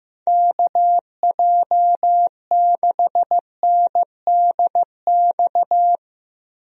Morse code, heard as a keyed tone, about 15 wpm, 700 Hz. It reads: KJ6NDX
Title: USA Call Signs - 6 Characters - 15wpm